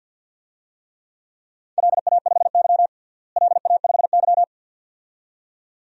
Morse code, heard as a keyed tone, about 51 wpm, 700 Hz. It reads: LA5Y LA5Y
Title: DXCC-100 Call Signs - 4 Characters - Repeated 2x - 50wpm